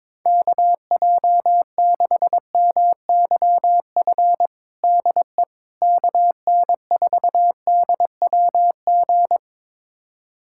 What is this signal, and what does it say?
Morse code, heard as a keyed tone, about 22 wpm, 700 Hz. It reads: KJ6MYF DE KN4DWG
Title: USA Callsign DE Callsign 22wpm